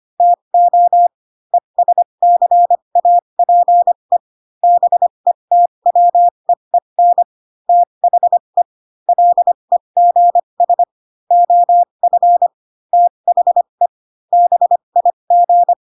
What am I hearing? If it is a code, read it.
Morse code, heard as a keyed tone, about 25 wpm, spaced as for 22 wpm, 700 Hz. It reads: TO ESCAPE BETWEEN THE LEGS OF THE BIG
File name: Pinocchio / Pinocchio ch3 18 25 ik1hge